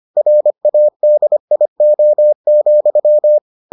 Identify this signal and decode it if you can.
Morse code, heard as a keyed tone, about 25 wpm, 600 Hz. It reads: RADIO,